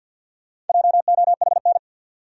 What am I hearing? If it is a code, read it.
Morse code, heard as a keyed tone, about 51 wpm, 700 Hz. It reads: JOHN
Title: USA Common Names - Max 4 Letters - 50wpm